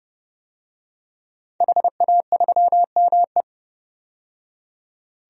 Morse code, heard as a keyed tone, about 32 wpm, 700 Hz. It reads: HA3ME
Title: DXCC-100 Call Signs - 5 Characters - 30wpm